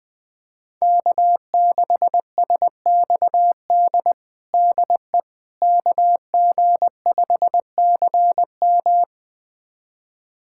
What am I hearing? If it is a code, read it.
Morse code, heard as a keyed tone, about 20 wpm, 700 Hz. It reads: K6SXD DE KG5CM
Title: USA Callsign DE Callsign - 5 Characters - 20wpm